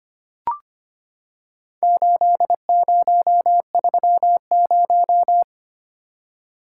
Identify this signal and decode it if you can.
Morse code, heard as a keyed tone, about 25 wpm, 700 Hz. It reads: 8030
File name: practice / 25wpm / Four Letter-Number Combinations - 25wpm